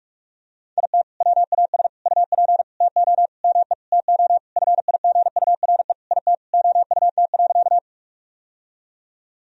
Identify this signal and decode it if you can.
Morse code, heard as a keyed tone, about 46 wpm, 700 Hz. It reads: IT WAS UP TO ME TO FIGURE IT OUT.
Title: Sentences from Top 1000 Words - Review Part 2 - 45wpm